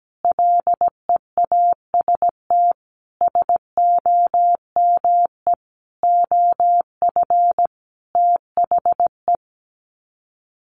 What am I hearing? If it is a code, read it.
Morse code, heard as a keyed tone, about 17 wpm, 700 Hz. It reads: LEAST SOME OF THE
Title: Sets of 5 Words Encore 17wpm